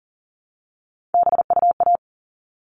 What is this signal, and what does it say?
Morse code, heard as a keyed tone, about 40 wpm, 700 Hz. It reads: BUA
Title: All Cut Numbers - 3-Digit 40wpm